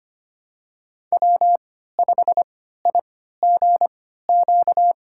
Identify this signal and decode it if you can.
Morse code, heard as a keyed tone, about 25 wpm, spaced as for 14 wpm, 700 Hz. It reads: W5IGQ
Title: USA Call Signs - 3x Character Spacing - 25wpm